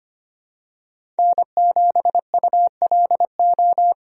Morse code, heard as a keyed tone, about 25 wpm, 700 Hz. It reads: N7ULO
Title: USA Call Signs - 5 Characters - 25wpm